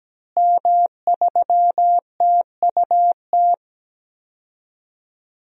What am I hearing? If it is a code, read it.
Morse code, heard as a keyed tone, about 17 wpm, 700 Hz. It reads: M3TUT
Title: DXCC-100 Call Signs - 5 Characters - Repeated 2x - 17wpm